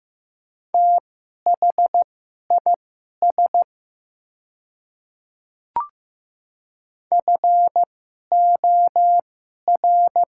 Morse code, heard as a keyed tone, about 15 wpm, spaced as for 11 wpm, 700 Hz. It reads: THIS FOR
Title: 2x Character Spacing - Top 200 Words - 15wpm